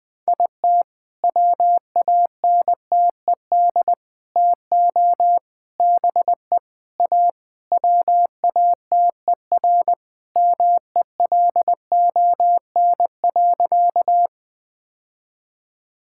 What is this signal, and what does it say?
Morse code, heard as a keyed tone, about 21 wpm, 700 Hz. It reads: IT WANTED TO BE A WATER MELON.
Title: All-Age Appropriate Jokes - 20wpm